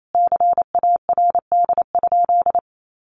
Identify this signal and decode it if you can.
Morse code, heard as a keyed tone, about 28 wpm, 700 Hz. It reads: CARD?
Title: Sentences from Top 1400 Words 28wpm